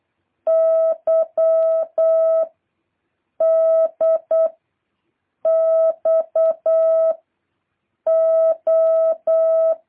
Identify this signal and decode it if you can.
Morse code, heard as a keyed tone, about 8 wpm, 650 Hz. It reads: Y D X O
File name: jh8cbh / 2023cw